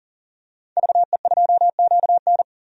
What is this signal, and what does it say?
Morse code, heard as a keyed tone, about 37 wpm, 700 Hz. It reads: VE2QN
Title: DXCC-100 Call Signs - 5 Characters - 40wpm